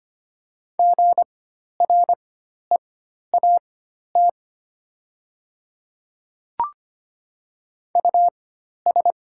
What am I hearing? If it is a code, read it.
Morse code, heard as a keyed tone, about 25 wpm, spaced as for 12 wpm, 700 Hz. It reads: GREAT US
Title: 4x Character Spacing - Top 300 Words - 25wpm